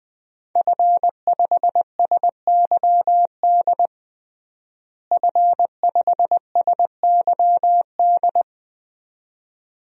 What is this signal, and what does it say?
Morse code, heard as a keyed tone, about 20 wpm, 700 Hz. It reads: F5SYD F5SYD
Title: DXCC-100 Call Signs - 5 Characters - Repeated 2x - 20wpm